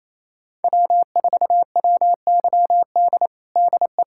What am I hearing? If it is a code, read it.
Morse code, heard as a keyed tone, about 28 wpm, 700 Hz. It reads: W4WYD DE
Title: USA Callsign DE Callsign - 5 Characters - 28wpm